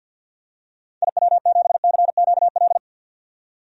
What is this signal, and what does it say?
Morse code, heard as a keyed tone, about 53 wpm, 700 Hz. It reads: IW7CQL
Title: DXCC-100 Call Signs - 6 Characters - 50wpm